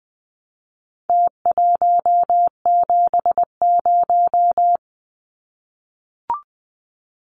Morse code, heard as a keyed tone, about 20 wpm, 700 Hz. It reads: T170
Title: Numbers - Contest Style - 4-Digit 20wpm